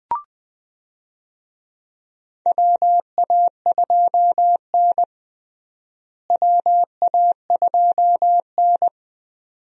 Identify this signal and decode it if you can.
Morse code, heard as a keyed tone, about 20 wpm, 700 Hz. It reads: WA2N WA2N
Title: USA Call Signs - 4 Characters - Repeated 2x - 20wpm